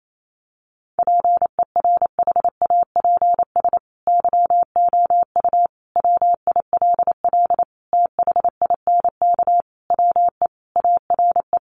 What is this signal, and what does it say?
Morse code, heard as a keyed tone, about 28 wpm, 700 Hz. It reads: PERHAPS YOU WILL THINK WE ARE
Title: Sentences from Top 1400 Words 28wpm